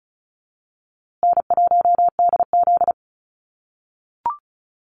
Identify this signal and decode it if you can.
Morse code, heard as a keyed tone, about 34 wpm, 700 Hz. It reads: N1DZ
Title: USA Call Signs Repeated 2x 35wpm